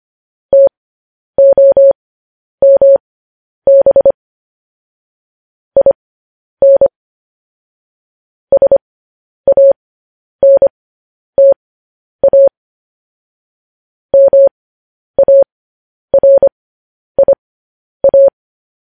Morse code, heard as a keyed tone, about 25 wpm, spaced as for 10 wpm, 550 Hz. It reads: TOMB IN SANTA MARIA
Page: news headlines in morse code at 10 wpm